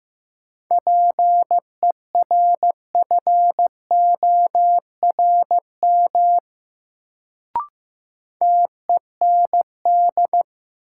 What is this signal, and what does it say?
Morse code, heard as a keyed tone, about 15 wpm, 700 Hz. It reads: PERFORM TEND
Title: Sentences from Top 1000 Words - Review Part 5 - New Word Drill - 15wpm